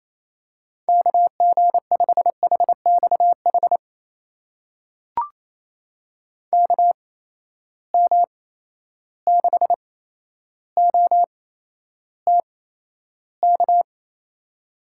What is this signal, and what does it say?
Morse code, heard as a keyed tone, about 28 wpm, spaced as for 12 wpm, 700 Hz. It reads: KG5HXH KM6OTK
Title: USA Call Signs - 6 Characters - 8x Character Spacing - 28wpm